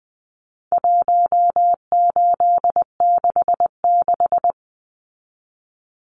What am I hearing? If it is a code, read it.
Morse code, heard as a keyed tone, about 20 wpm, 700 Hz. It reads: 1866
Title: Numbers - 4-Digit 20wpm